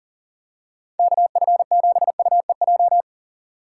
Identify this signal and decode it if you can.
Morse code, heard as a keyed tone, about 40 wpm, 700 Hz. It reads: KF7UEJ
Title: USA Call Signs - 6 Characters - 6x Character Spacing - 40wpm